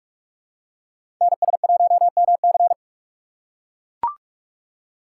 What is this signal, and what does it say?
Morse code, heard as a keyed tone, about 44 wpm, 700 Hz. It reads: NS1MC
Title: USA Call Signs - 5 Characters - 3x Character Spacing - 45wpm